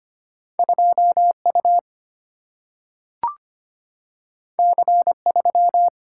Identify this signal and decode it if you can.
Morse code, heard as a keyed tone, about 25 wpm, 700 Hz. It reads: 2U C3
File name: Two Letter-Number Combinations - 25wpm